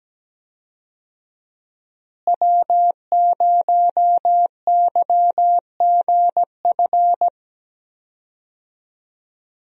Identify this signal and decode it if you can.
Morse code, heard as a keyed tone, about 17 wpm, 700 Hz. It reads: W0YGF
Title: USA Call Signs - 5 Characters - Repeated 2x - 17wpm